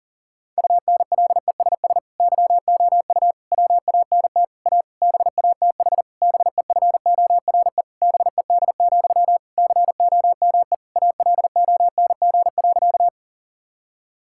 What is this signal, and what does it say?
Morse code, heard as a keyed tone, about 40 wpm, 700 Hz. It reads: UNLESS YOU WANT A BATH BEFORE BED, COME ALONG.
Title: Sentences from Top 1600 Words 40wpm